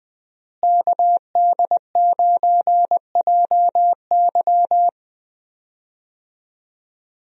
Morse code, heard as a keyed tone, about 20 wpm, 700 Hz. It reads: KD9JY
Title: USA Call Signs - 5 Characters - Repeated 2x - 20wpm